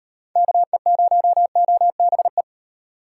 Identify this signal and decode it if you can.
Morse code, heard as a keyed tone, about 37 wpm, 700 Hz. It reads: KE0OBE